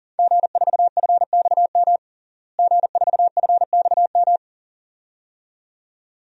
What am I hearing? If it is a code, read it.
Morse code, heard as a keyed tone, about 40 wpm, 700 Hz. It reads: G4FXM G4FXM
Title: DXCC-100 Call Signs - 5 Characters - Repeated 2x - 40wpm